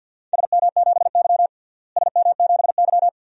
Morse code, heard as a keyed tone, about 52 wpm, 700 Hz. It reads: SM7Y SM7Y
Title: DXCC-100 Call Signs - 4 Characters - Repeated 2x - 50wpm